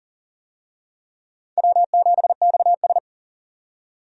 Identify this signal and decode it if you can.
Morse code, heard as a keyed tone, about 37 wpm, 700 Hz. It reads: W7XS
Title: USA Call Signs - 4 Characters - 40wpm